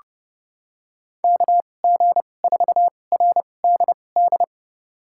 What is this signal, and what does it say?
Morse code, heard as a keyed tone, about 32 wpm, spaced as for 22 wpm, 700 Hz. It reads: KG4RDD
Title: USA Call Signs - 6 Characters - 2x Character Spacing - 30wpm